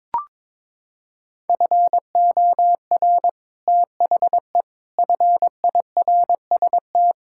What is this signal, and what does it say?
Morse code, heard as a keyed tone, about 22 wpm, 700 Hz. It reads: FOR THE FIRST
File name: practice / 22wpm / Sets of 7 Words - 22wpm